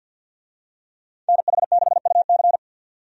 Morse code, heard as a keyed tone, about 51 wpm, 700 Hz. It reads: NH6UC
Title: USA Call Signs - 5 Characters - 50wpm